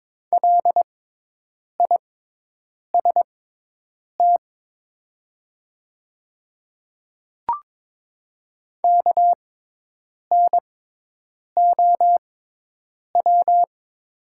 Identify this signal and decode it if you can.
Morse code, heard as a keyed tone, about 22 wpm, spaced as for 8 wpm, 700 Hz. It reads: LIST KNOW
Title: 6x Character Spacing - Top 500 Words - 22wpm